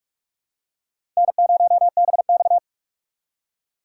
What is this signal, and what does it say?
Morse code, heard as a keyed tone, about 38 wpm, 700 Hz. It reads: N0BX